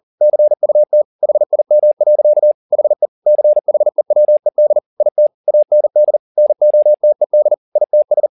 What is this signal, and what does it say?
Morse code, heard as a keyed tone, about 40 wpm, 600 Hz. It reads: CUT HIM. HE CHEWED IT AND NOTED ITS
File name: the old man and the sea 2 40wpm